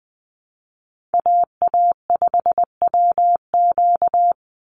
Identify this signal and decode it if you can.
Morse code, heard as a keyed tone, about 19 wpm, 700 Hz. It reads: AA5WQ
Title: USA Call Signs 20wpm